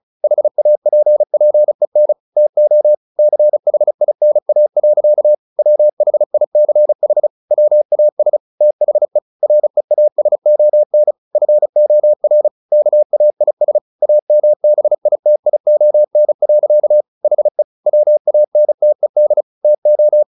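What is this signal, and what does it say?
Morse code, heard as a keyed tone, about 34 wpm, 600 Hz. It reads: HAPPEN TO CHINA. WHICH WAS THE REASON FOR KAIS AMBITION. HE WANTED TO
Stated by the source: Morse practice at 35 wpm from CW med Gnister, sending Ken Follett's novel Never in English.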